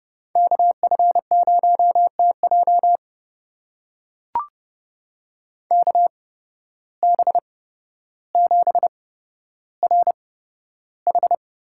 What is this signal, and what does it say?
Morse code, heard as a keyed tone, about 31 wpm, spaced as for 13 wpm, 700 Hz. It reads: KF0TJ KB7RH
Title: USA Call Signs - 5 Characters - 8x Character Spacing - 30wpm